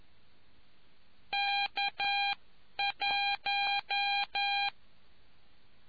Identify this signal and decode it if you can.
Morse code, heard as a keyed tone, about 11 wpm, spaced as for 9 wpm, 800 Hz. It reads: K1